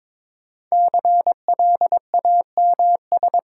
Morse code, heard as a keyed tone, about 23 wpm, 690 Hz. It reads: CLAMS
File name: Difficult Words with Hints - 22wpm